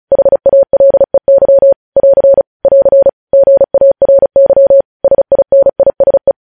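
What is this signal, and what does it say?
Morse code, heard as a keyed tone, about 36 wpm, 550 Hz. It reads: HALEY <AR> <AR> GARY SINISE
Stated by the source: Morse news headlines practice at 35 words per minute.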